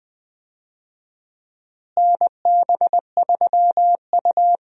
Morse code, heard as a keyed tone, about 21 wpm, 700 Hz. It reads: NB3U
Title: USA Call Signs - 4 Characters - Repeated 2x - 20wpm